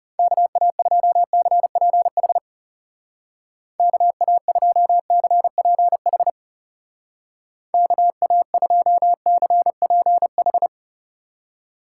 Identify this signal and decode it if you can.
Morse code, heard as a keyed tone, about 39 wpm, 700 Hz. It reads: KA2CPH KA2CPH KA2CPH